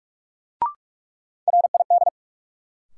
Morse code, heard as a keyed tone, about 46 wpm, 700 Hz. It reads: RID
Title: Sets of 4 Words - New Word Drill - 45wpm